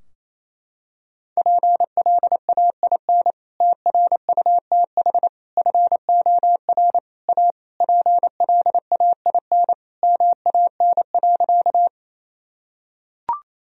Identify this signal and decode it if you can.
Morse code, heard as a keyed tone, about 28 wpm, 700 Hz. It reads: PLAIN TRUTH FOR A PLAIN MAN.
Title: Sentences from Top 1000 Words - Review Part 2 - 28wpm